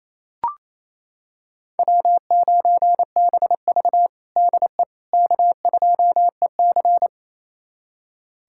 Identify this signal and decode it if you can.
Morse code, heard as a keyed tone, about 28 wpm, 700 Hz. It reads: W9BV DE K2EC
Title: USA Callsign DE Callsign - 4 Characters - 28wpm